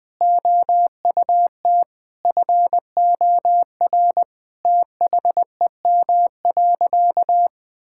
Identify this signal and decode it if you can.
Morse code, heard as a keyed tone, about 21 wpm, 700 Hz. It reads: OUT FOR THEM.
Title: Sentences with 7-Word Phrases - 20wpm